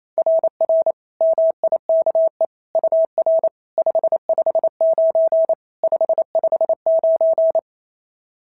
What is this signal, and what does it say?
Morse code, heard as a keyed tone, about 27 wpm, 660 Hz. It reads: RR MIKE UR 559 559
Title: SOTA QSO Practice 28wpm